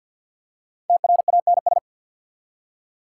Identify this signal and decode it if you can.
Morse code, heard as a keyed tone, about 54 wpm, 690 Hz. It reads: TRANS
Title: Prefixes and Suffixes 50wpm